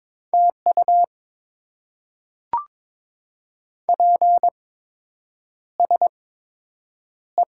Morse code, heard as a keyed tone, about 22 wpm, spaced as for 8 wpm, 700 Hz. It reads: TU PSE
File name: Top 90 QSO Elements - 8x Character Spacing - 22wpm